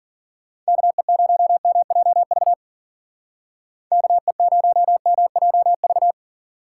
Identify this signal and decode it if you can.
Morse code, heard as a keyed tone, about 48 wpm, 700 Hz. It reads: KE0MJV KE0MJV
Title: Speed Racing - USA Call Signs 40wpm